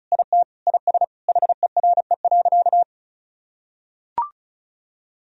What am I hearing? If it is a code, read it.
Morse code, heard as a keyed tone, about 36 wpm, 700 Hz. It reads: IT IS HERE.
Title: Sentences from Top 1000 Words - Review Part 2 - 35wpm